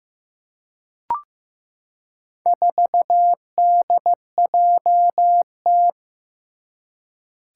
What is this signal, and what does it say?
Morse code, heard as a keyed tone, about 15 wpm, 700 Hz. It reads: 4DJT
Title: Four Challenging Letter-Number Combinations 15wpm